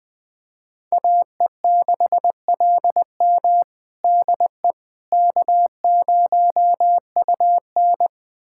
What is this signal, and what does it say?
Morse code, heard as a keyed tone, about 21 wpm, 700 Hz. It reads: AE6LM DE K0UN
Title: USA Callsign DE Callsign 20wpm